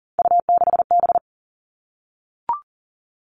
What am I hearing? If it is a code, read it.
Morse code, heard as a keyed tone, about 40 wpm, 700 Hz. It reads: U6B
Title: All Cut Numbers - 3-Digit 40wpm